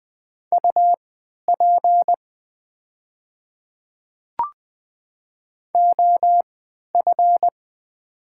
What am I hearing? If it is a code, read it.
Morse code, heard as a keyed tone, about 20 wpm, spaced as for 11 wpm, 700 Hz. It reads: UP OF
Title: 3x Character Spacing - Top 100 Words - 20wpm